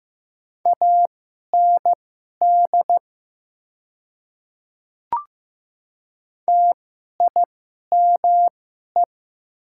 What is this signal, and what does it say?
Morse code, heard as a keyed tone, about 15 wpm, spaced as for 11 wpm, 700 Hz. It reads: AND TIME
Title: 2x Character Spacing - Top 100 Words - 15wpm